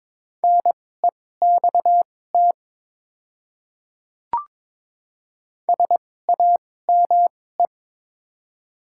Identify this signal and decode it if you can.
Morse code, heard as a keyed tone, about 22 wpm, spaced as for 16 wpm, 700 Hz. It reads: NEXT SAME
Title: 2x Character Spacing - Top 500 Words - 22wpm